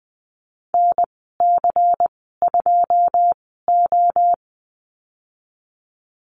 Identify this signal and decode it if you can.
Morse code, heard as a keyed tone, about 20 wpm, spaced as for 14 wpm, 700 Hz. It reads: NC2O
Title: 161 Lesson 26g - Letter g - Call Signs 20wpm